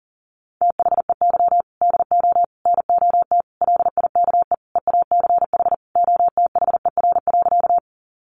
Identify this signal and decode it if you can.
Morse code, heard as a keyed tone, about 40 wpm, 700 Hz. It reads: THEY DO NOT LIKE EACH OTHER.
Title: Easy Phrases 40wpm